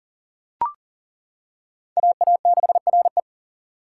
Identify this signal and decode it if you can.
Morse code, heard as a keyed tone, about 40 wpm, 700 Hz. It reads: AA6RE
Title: K1USN SST QSO Practice - 40wpm